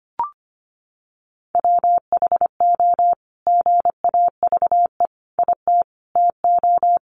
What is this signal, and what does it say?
Morse code, heard as a keyed tone, about 25 wpm, 700 Hz. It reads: WHO GAVE IT TO
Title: Sentences from Top 500 Words 25wpm